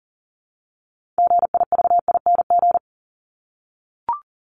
Morse code, heard as a keyed tone, about 40 wpm, 700 Hz. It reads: GIVING
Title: ING Suffix - Top 100 Words with Base Words 40wpm